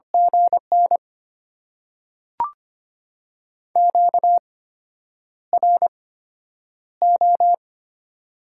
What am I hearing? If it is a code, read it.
Morse code, heard as a keyed tone, about 25 wpm, spaced as for 9 wpm, 690 Hz. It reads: GN QRO
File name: Top 90 QSO Elements - 8x Character Spacing - 25wpm